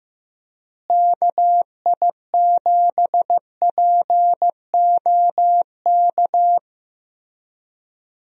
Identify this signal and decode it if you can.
Morse code, heard as a keyed tone, about 16 wpm, 700 Hz. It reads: KI7POK
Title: USA Call Signs - 6 Characters - Repeated 3x - 15wpm